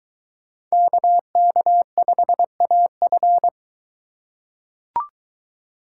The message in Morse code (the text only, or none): KK5AF